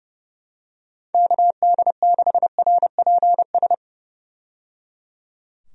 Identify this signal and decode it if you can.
Morse code, heard as a keyed tone, about 30 wpm, 700 Hz. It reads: KD6RPS
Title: USA Call Signs - 6 Characters - 30wpm